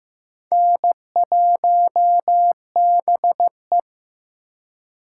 Morse code, heard as a keyed tone, about 15 wpm, 700 Hz. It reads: N1BE
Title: USA Call Signs - 4 Characters - 3x Character Spacing - 15wpm